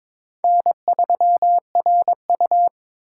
Morse code, heard as a keyed tone, about 22 wpm, 700 Hz. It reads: N3RU
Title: USA Call Signs - 4 Characters - 2x Character Spacing - 22wpm